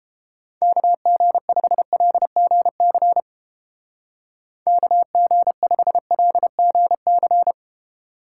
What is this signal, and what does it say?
Morse code, heard as a keyed tone, about 30 wpm, 700 Hz. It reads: KG5LGC KG5LGC